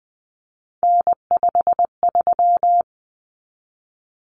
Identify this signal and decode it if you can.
Morse code, heard as a keyed tone, about 20 wpm, 700 Hz. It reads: N53
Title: Common Cut Numbers - 3-Digit 20wpm